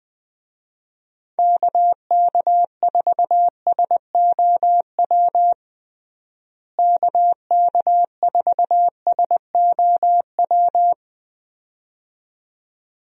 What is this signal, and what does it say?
Morse code, heard as a keyed tone, about 20 wpm, 700 Hz. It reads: KK4SOW KK4SOW
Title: USA Call Signs - 6 Characters - Repeated 2x - 20wpm